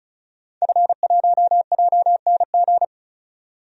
Morse code, heard as a keyed tone, about 34 wpm, 700 Hz. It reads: F1JNG
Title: DXCC-100 Call Signs - 5 Characters - 35wpm